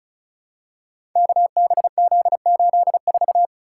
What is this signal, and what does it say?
Morse code, heard as a keyed tone, about 36 wpm, 700 Hz. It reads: KBZ84
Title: Code Groups - 5 Letter:Numbers - 35wpm